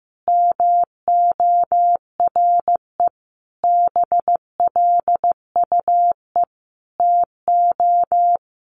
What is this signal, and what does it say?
Morse code, heard as a keyed tone, about 15 wpm, 700 Hz. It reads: MORE BLUE TO
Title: Sentences from Top 1200 Words 15wpm